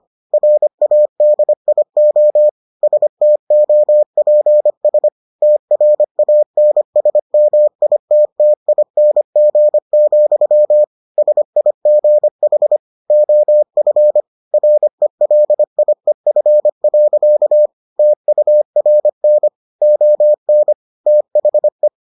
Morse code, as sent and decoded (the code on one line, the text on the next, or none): .-. .- -.. .. --- / ... - --- .--. ... / - .-. .- -. ... -- .. - - .. -. --. --..-- / ... .. --. .... / --- ..-. / .-. .-.. .. ..-. .-.-.- / - ..- .-. -. / --- -. / - .... .
RADIO STOPS TRANSMITTING, SIGH OF RELIEF. TURN ON THE